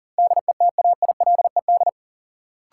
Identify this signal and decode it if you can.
Morse code, heard as a keyed tone, about 40 wpm, 700 Hz. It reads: DETAILED
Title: Sentences from Top 1500 Words - New Word Drill Alt - 40wpm